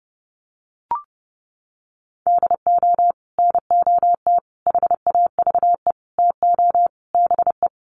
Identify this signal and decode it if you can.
Morse code, heard as a keyed tone, about 30 wpm, 700 Hz. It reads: DO NOT HAVE TO BE
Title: Sets of 5 Words 30wpm